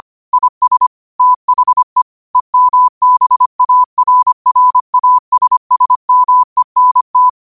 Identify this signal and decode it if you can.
Morse code, heard as a keyed tone, about 25 wpm, 1000 Hz. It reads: IS THE EMBARRASSMENT
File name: qotdfiles / 25wpm